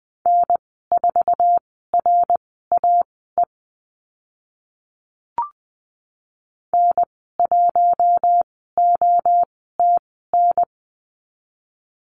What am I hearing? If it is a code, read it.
Morse code, heard as a keyed tone, about 20 wpm, spaced as for 15 wpm, 700 Hz. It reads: N4RAE N1OTN
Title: 062 Lesson 11g - Letter h - Call Signs 20wpm